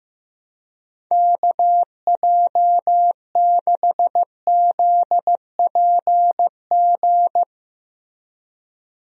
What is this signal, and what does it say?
Morse code, heard as a keyed tone, about 15 wpm, 700 Hz. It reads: KJ6ZPG
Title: USA Call Signs - 6 Characters - Repeated 2x - 15wpm